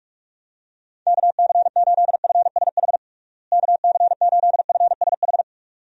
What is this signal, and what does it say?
Morse code, heard as a keyed tone, about 42 wpm, 700 Hz. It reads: KC8FSH KC8FSH
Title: USA Call Signs - 6 Characters - Repeated 2x - 45wpm